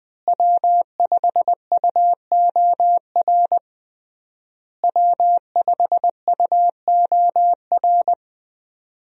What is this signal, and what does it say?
Morse code, heard as a keyed tone, about 20 wpm, 700 Hz. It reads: W5UOR W5UOR
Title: USA Call Signs - 5 Characters - Repeated 3x - 20wpm